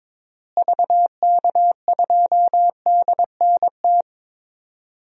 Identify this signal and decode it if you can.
Morse code, heard as a keyed tone, about 22 wpm, 700 Hz. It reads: VK2DNT
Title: DXCC-100 Call Signs - 6 Characters - Repeated 3x - 22wpm